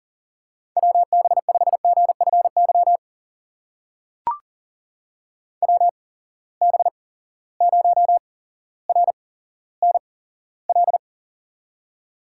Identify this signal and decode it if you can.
Morse code, heard as a keyed tone, about 40 wpm, spaced as for 17 wpm, 700 Hz. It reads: WB5GFY WB0RNL
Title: USA Call Signs - 6 Characters - 8x Character Spacing - 40wpm